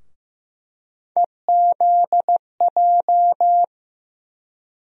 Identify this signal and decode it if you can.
Morse code, heard as a keyed tone, about 15 wpm, 700 Hz. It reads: EZJ
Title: USA Call Sign Postfixes - 3 Letters 15wpm